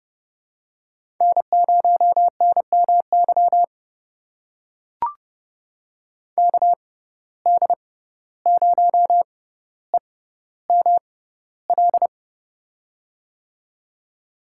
Morse code, heard as a keyed tone, about 30 wpm, spaced as for 15 wpm, 700 Hz. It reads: N0NMY KD0EML
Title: USA Call Signs - 6x Character Spacing - 30wpm